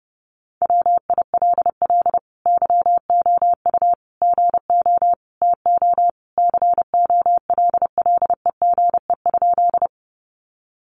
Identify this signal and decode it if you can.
Morse code, heard as a keyed tone, about 28 wpm, 700 Hz. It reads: WILL YOU GO TO COLLEGE?
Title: Sentences from Top 1300 Words 30wpm